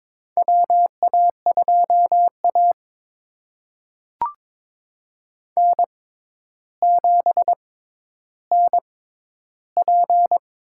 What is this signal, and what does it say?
Morse code, heard as a keyed tone, about 22 wpm, spaced as for 11 wpm, 700 Hz. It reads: WA2A N7NP
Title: USA Call Signs - 4 Characters - 6x Character Spacing - 22wpm